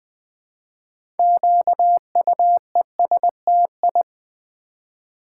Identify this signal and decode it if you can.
Morse code, heard as a keyed tone, about 20 wpm, 700 Hz. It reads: QUESTI
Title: Build A Word - 3 Letters At A Time - 8-Letter Words - 20wpm